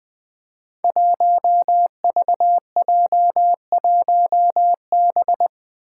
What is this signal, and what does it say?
Morse code, heard as a keyed tone, about 20 wpm, 700 Hz. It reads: 1VJ1B
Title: Five Challenging Letter-Number Combinations 20wpm